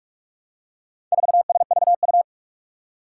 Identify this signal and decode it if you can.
Morse code, heard as a keyed tone, about 44 wpm, 700 Hz. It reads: 4SVU
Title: Four Challenging Letter-Number Combinations 45wpm